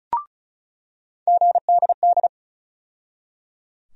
Morse code, heard as a keyed tone, about 34 wpm, 700 Hz. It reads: GDD
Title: USA Call Sign Postfixes - 3 Letters 35wpm